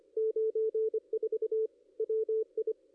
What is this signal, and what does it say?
Morse code, heard as a keyed tone, about 25 wpm, 440 Hz. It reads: 94 WI
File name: ss-cw-k5tr-op-n5rz